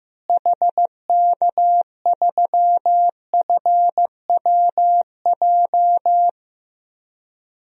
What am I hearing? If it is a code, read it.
Morse code, heard as a keyed tone, about 15 wpm, 700 Hz. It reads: HK3FWJ
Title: DXCC-100 Call Signs - 6 Characters - Repeated 2x - 15wpm